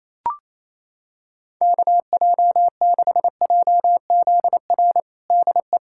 Morse code, heard as a keyed tone, about 28 wpm, 700 Hz. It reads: KJ6JZR DE
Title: USA Callsign DE Callsign 28wpm